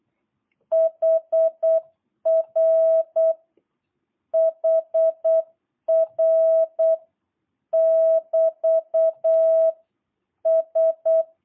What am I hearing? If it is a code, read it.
Morse code, heard as a keyed tone, about 8 wpm, 650 Hz. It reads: HR HR <BT> S